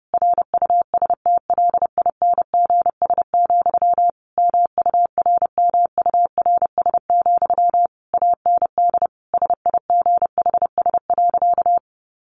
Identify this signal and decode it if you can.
Morse code, heard as a keyed tone, about 30 wpm, 700 Hz. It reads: RUSTLINGS, MURMURS, AND SIGHS.
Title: A Sound Of Thunder 30wpm